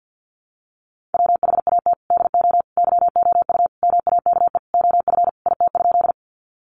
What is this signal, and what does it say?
Morse code, heard as a keyed tone, about 50 wpm, 700 Hz. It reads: WHAT DO YOU MAKE OF IT?
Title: Sentences from Top 100 Words 50wpm